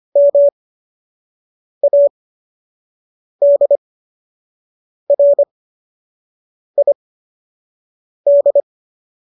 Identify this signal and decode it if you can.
Morse code, heard as a keyed tone, about 25 wpm, spaced as for 6 wpm, 570 Hz. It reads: MADRID